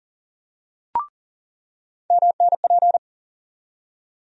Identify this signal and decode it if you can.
Morse code, heard as a keyed tone, about 40 wpm, 700 Hz. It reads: MNP
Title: USA Call Sign Postfixes - 3 Letters 40wpm